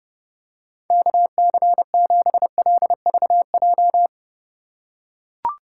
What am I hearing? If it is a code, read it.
Morse code, heard as a keyed tone, about 30 wpm, 700 Hz. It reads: KC7LVJ